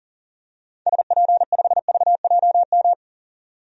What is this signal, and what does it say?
Morse code, heard as a keyed tone, about 37 wpm, 700 Hz. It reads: SP5VJM
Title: DXCC-100 Call Signs - 6 Characters - 40wpm